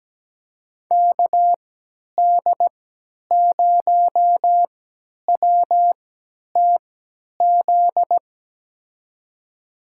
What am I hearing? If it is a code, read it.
Morse code, heard as a keyed tone, about 17 wpm, spaced as for 10 wpm, 700 Hz. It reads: KD0WTZ
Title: USA Call Signs - 3x Character Spacing - 17wpm